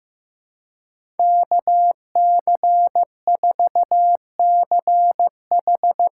Morse code, heard as a keyed tone, about 15 wpm, 700 Hz. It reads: KC4CH
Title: USA Call Signs - 6x Character Spacing - 15wpm